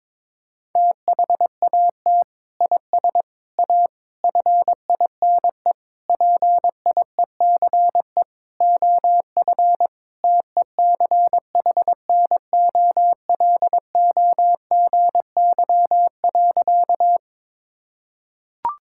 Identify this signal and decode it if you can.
Morse code, heard as a keyed tone, about 22 wpm, 700 Hz. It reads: THAT IS A FINE PIECE OF TECHNOLOGY.
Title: Sentences from Top 1000 Words - Review Part 5 - 22wpm